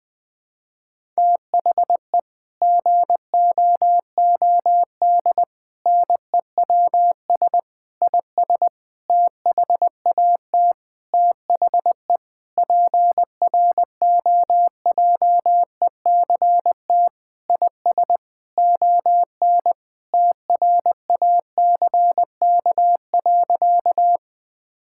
Morse code, heard as a keyed tone, about 20 wpm, 700 Hz. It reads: THE GOOD NEWS IS THAT THE PROJECT IS ON TRACK.
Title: Sentences with 6-Word Phrases - Spoken Beginning - 20wpm